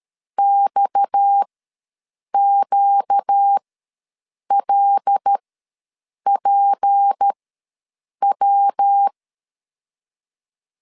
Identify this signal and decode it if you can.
Morse code, heard as a keyed tone, about 13 wpm, spaced as for 7 wpm, 800 Hz. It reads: XQLPW
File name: LEZIONI / 10 SIGNS / 10A SLASH ws 12wpm 3443